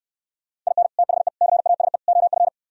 Morse code, heard as a keyed tone, about 68 wpm, 700 Hz. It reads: IN THE CASE OF